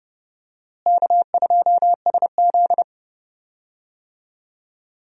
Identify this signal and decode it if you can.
Morse code, heard as a keyed tone, about 31 wpm, 700 Hz. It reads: K2SZ
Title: USA Call Signs - 4 Characters - Repeated 2x - 30wpm